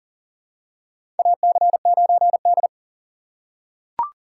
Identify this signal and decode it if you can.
Morse code, heard as a keyed tone, about 40 wpm, 700 Hz. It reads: AC9D
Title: USA Call Signs - 4 Characters - 2x Character Spacing - 40wpm